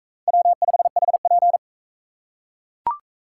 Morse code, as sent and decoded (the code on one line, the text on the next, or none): .-- ..... .... .--.
W5HP